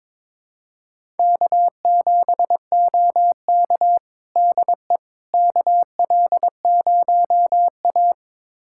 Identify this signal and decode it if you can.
Morse code, heard as a keyed tone, about 22 wpm, 700 Hz. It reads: K7OK DE KL0A
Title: USA Callsign DE Callsign - 4 Characters - 22wpm